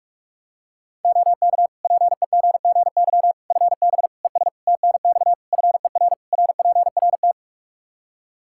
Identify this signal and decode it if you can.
Morse code, heard as a keyed tone, about 41 wpm, 700 Hz. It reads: OK PEGGY FB ES TNX FER RPRT
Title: Practice QSO - Exchange 2 - Info - 45wpm